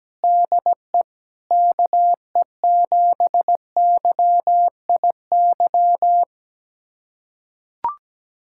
Morse code, heard as a keyed tone, about 17 wpm, 700 Hz. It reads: DE KE7YIY
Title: USA Callsign DE Callsign - 6 Characters - 17wpm